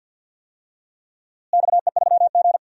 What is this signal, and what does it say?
Morse code, heard as a keyed tone, about 47 wpm, 700 Hz. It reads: XE3G
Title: DXCC-100 Call Signs - 4 Characters - Repeated 2x - 50wpm